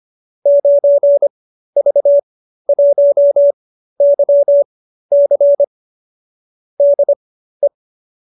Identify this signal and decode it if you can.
Morse code, heard as a keyed tone, about 25 wpm, spaced as for 13 wpm, 590 Hz. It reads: 9V1YC DE